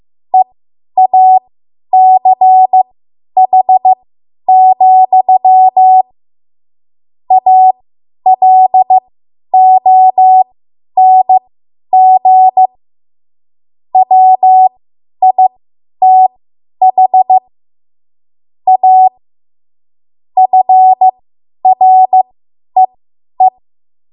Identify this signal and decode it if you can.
Morse code, heard as a keyed tone, about 15 wpm, spaced as for 10 wpm, 750 Hz. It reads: EACH, ALONG WITH A FREE